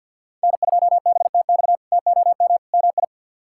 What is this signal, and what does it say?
Morse code, heard as a keyed tone, about 52 wpm, 700 Hz. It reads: N2BTX TOM MI
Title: K1USN SST QSO Practice - 50wpm